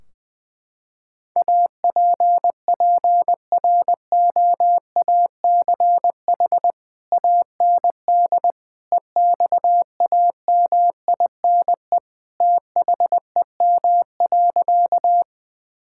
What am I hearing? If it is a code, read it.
Morse code, heard as a keyed tone, about 20 wpm, 700 Hz. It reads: APPROACH AND EXAMINE THEM.
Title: Sentences from Top 1600 Words 20wpm